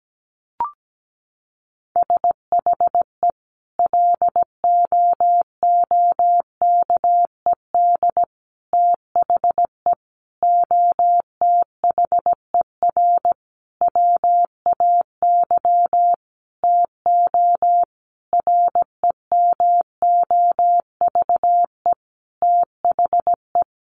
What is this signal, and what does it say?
Morse code, heard as a keyed tone, about 17 wpm, 700 Hz. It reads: SHE LOOKED THE OTHER WAY TO REMOVE THE
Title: Sentences from Top 800 Words 17wpm